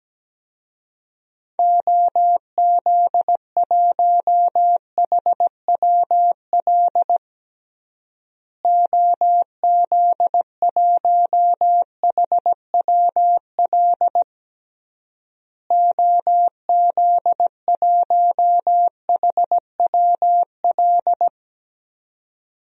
Morse code, heard as a keyed tone, about 17 wpm, 700 Hz. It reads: OZ1HWL OZ1HWL OZ1HWL
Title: DXCC-100 Call Signs - 6 Characters - Repeated 3x - 17wpm